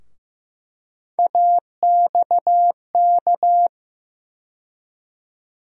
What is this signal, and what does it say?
Morse code, heard as a keyed tone, about 15 wpm, 700 Hz. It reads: AXK
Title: USA Call Sign Postfixes - 3 Letters 15wpm